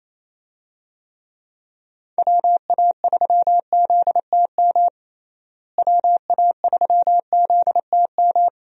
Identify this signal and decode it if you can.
Morse code, heard as a keyed tone, about 28 wpm, 700 Hz. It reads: WA3ZTM WA3ZTM
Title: USA Call Signs - 6 Characters - Repeated 2x - 28wpm